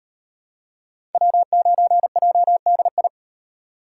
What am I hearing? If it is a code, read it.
Morse code, heard as a keyed tone, about 38 wpm, 700 Hz. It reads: W9JDI